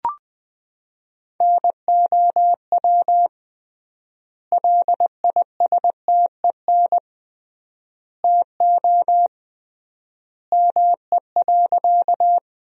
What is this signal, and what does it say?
Morse code, heard as a keyed tone, about 20 wpm, 700 Hz. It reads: NOW LISTEN TO ME.
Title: 3x Word-Spacing - Sentences from Top 400 Words - 20wpm